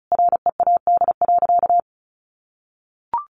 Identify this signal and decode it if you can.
Morse code, heard as a keyed tone, about 34 wpm, 700 Hz. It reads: READ.
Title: Sentences from Top 1000 Words 35wpm